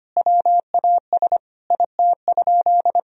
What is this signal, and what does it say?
Morse code, heard as a keyed tone, about 25 wpm, 700 Hz. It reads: WAS IT?
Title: Sentences from Top 1000 Words - Review Part 4 - 25wpm